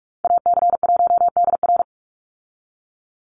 Morse code, heard as a keyed tone, about 44 wpm, 700 Hz. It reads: AC1DR
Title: USA Call Signs 45wpm